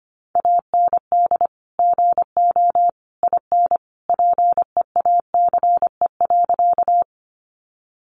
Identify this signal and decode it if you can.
Morse code, heard as a keyed tone, about 25 wpm, 700 Hz. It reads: AND GO IN PEACE.
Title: Sentences from Top 1500 Words 25wpm